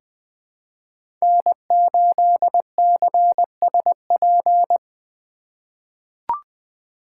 Morse code, heard as a keyed tone, about 20 wpm, 700 Hz. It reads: N8CSP
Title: USA Call Signs - 5 Characters - 3x Character Spacing - 20wpm